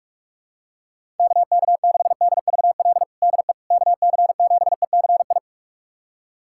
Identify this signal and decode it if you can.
Morse code, heard as a keyed tone, about 46 wpm, 700 Hz. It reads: KK6DVL DE KC7ECI
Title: USA Callsign DE Callsign - 6 Characters - 45wpm